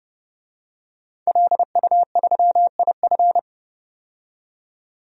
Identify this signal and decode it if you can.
Morse code, heard as a keyed tone, about 30 wpm, 700 Hz. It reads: LU3IF
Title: DXCC-100 Call Signs - 5 Characters - 30wpm